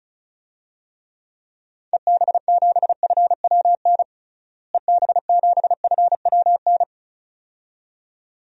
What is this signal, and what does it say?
Morse code, heard as a keyed tone, about 36 wpm, 700 Hz. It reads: EB7FWN EB7FWN
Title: DXCC-100 Call Signs - 6 Characters - Repeated 2x - 35wpm